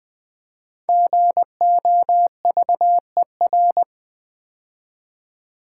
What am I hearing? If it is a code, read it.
Morse code, heard as a keyed tone, about 20 wpm, 700 Hz. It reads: GOVER
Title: Build A Word - 5 Letters At A Time - 10-Letter Words - 20wpm